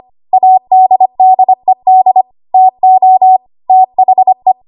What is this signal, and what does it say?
Morse code, heard as a keyed tone, about 25 wpm, 750 Hz. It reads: ADDED TO THE
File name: Archive / 25 WPM / 25WPM